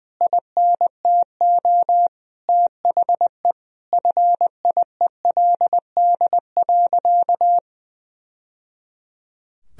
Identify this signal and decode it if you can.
Morse code, heard as a keyed tone, about 20 wpm, 700 Hz. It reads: INTO THE FIELD.
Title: Sentences from Top 1000 Words - Review Part 2 - 20wpm